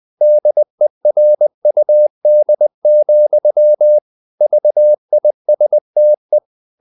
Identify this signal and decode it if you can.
Morse code, heard as a keyed tone, about 20 wpm, 600 Hz. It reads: DERUD, VISTE
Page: Ferie 20 wpm